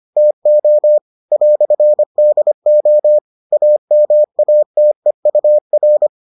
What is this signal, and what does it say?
Morse code, heard as a keyed tone, about 25 wpm, 600 Hz. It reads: TO "DO AMATEUR